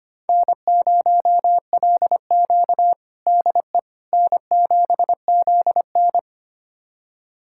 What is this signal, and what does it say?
Morse code, heard as a keyed tone, about 25 wpm, 700 Hz. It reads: N0LQ DE N7ZN
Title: USA Callsign DE Callsign - 4 Characters - 25wpm